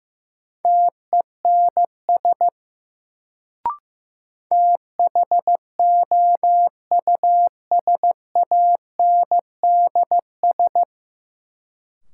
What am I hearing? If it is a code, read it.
Morse code, heard as a keyed tone, about 15 wpm, 700 Hz. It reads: TENS THOUSANDS
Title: Sets of 4 Words - New Word Drill - 15wpm